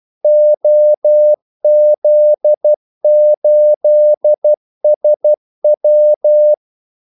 Morse code, heard as a keyed tone, about 12 wpm, 600 Hz. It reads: OZ8SW
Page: 2 ekstra ordmellemrum mellem kaldesignalerne